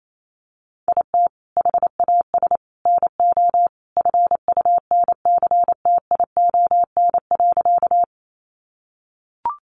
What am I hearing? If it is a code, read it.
Morse code, heard as a keyed tone, about 28 wpm, 700 Hz. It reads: IT HAS NO FUNCTION.
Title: Sentences from Top 1100 Words 28wpm